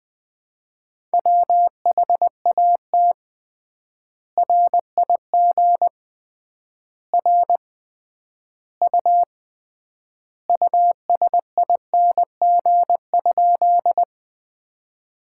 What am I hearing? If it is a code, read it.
Morse code, heard as a keyed tone, about 20 wpm, 700 Hz. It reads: WHAT RIG R U USING?
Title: Common QSO Phrases - 3x Word-Spacing - 20wpm